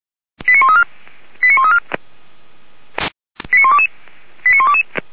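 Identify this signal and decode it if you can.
Five-tone selective call (ZVEI), heard as quick runs of five tones, tones of about 70 ms. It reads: 89134 89134 89122 89122
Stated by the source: ZVEI five-tone call, code 89103, received by SDR.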